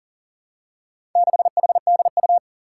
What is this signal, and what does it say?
Morse code, heard as a keyed tone, about 41 wpm, 700 Hz. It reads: 6HDU
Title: Four Challenging Letter-Number Combinations 40wpm